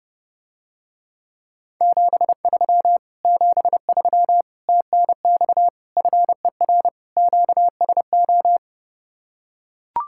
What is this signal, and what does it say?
Morse code, heard as a keyed tone, about 32 wpm, 700 Hz. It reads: 73 73 TNX FER QSO